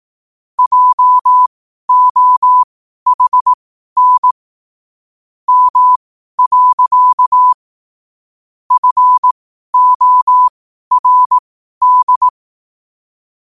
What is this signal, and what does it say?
Morse code, heard as a keyed tone, about 18 wpm, spaced as for 12 wpm, 1000 Hz. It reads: JOHN M. FORD
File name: qotdfiles / 12wpm